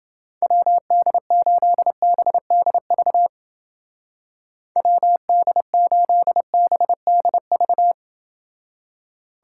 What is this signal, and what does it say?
Morse code, heard as a keyed tone, about 29 wpm, 700 Hz. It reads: WD8BDV WD8BDV